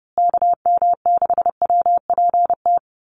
Morse code, heard as a keyed tone, about 30 wpm, 700 Hz. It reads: KM6WPT
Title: USA Call Signs Repeated 3x 30wpm